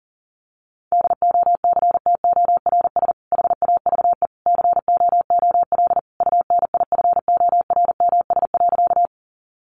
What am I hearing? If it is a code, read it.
Morse code, heard as a keyed tone, about 40 wpm, 700 Hz. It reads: DOCTORS HAVE COOL UNIFORMS.
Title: Easy Phrases 40wpm